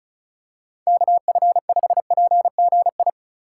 Morse code, heard as a keyed tone, about 34 wpm, 700 Hz. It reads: KF5PGI
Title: USA Call Signs - 6 Characters - 2x Character Spacing - 35wpm